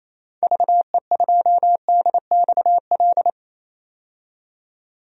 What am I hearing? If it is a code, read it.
Morse code, heard as a keyed tone, about 27 wpm, 700 Hz. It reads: VE2DXL
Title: DXCC-100 Call Signs - 6 Characters - 28wpm